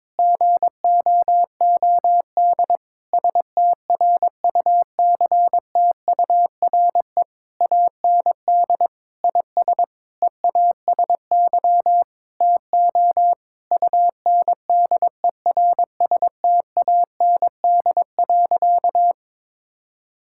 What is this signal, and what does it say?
Morse code, heard as a keyed tone, about 22 wpm, 700 Hz. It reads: GOOD STRUCTURE AND IS EASY TO UNDERSTAND.
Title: Sentences from Top 1000 Words - Review Part 6 - 22wpm